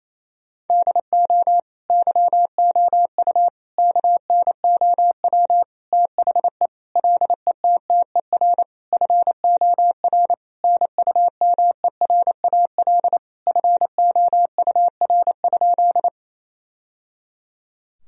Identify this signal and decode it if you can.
Morse code, heard as a keyed tone, about 28 wpm, 700 Hz. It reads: DO YOU KNOW THE LETTER FOR NUMERAL FOUR?
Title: Sentences from Top 1000 Words - Review Part 2 - 28wpm